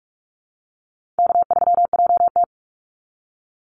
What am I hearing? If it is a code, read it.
Morse code, heard as a keyed tone, about 44 wpm, 700 Hz. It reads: K3JT
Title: Easy Call Signs Repeated 2x - 45wpm